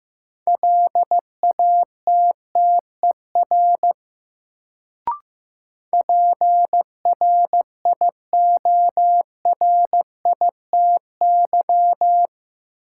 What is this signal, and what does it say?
Morse code, heard as a keyed tone, about 15 wpm, 700 Hz. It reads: LATTER PRIORITY
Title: Sentences from Top 1600 Words - New Word Drill Alt - 15wpm